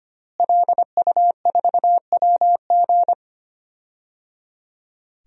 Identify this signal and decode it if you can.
Morse code, heard as a keyed tone, about 25 wpm, 700 Hz. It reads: LU4WG
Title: DXCC-100 Call Signs - 5 Characters - Repeated 3x - 25wpm